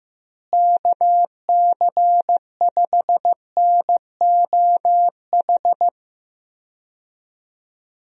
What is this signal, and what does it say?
Morse code, heard as a keyed tone, about 16 wpm, 700 Hz. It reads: KC5NOH
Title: USA Call Signs - 6 Characters - Repeated 3x - 15wpm